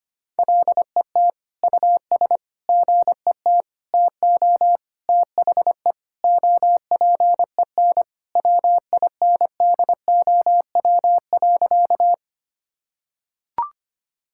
Morse code, heard as a keyed tone, about 25 wpm, 700 Hz. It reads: LET US GET TO THE OPEN WINDOW.
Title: Sentences from Top 1600 Words 25wpm